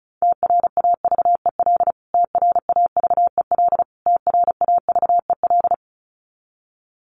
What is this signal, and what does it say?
Morse code, heard as a keyed tone, about 34 wpm, 700 Hz. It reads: TRAVEL TRAVEL TRAVEL
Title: Top 500 Words Repeated 35wpm